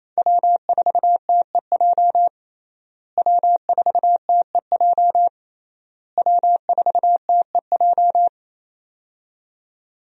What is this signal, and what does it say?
Morse code, heard as a keyed tone, about 28 wpm, 700 Hz. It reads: W4TEJ W4TEJ W4TEJ
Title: USA Call Signs - 5 Characters - Repeated 3x - 28wpm